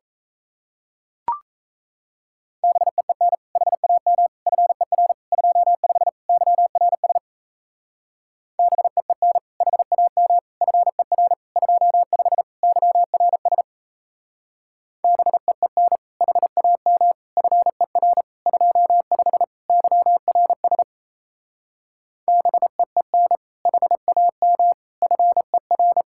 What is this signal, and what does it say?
Morse code, heard as a keyed tone, about 33 wpm, 700 Hz. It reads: BEEN HAM FER 25 YRS BEEN HAM FER 25 YRS BEEN HAM FER 25 YRS BEEN HAM FER
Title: Speed-Racing Common QSO Phrases Encore - 28wpm